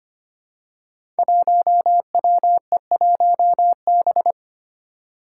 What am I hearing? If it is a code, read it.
Morse code, heard as a keyed tone, about 25 wpm, 700 Hz. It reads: 1WE1B
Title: Five Challenging Letter-Number Combinations 25wpm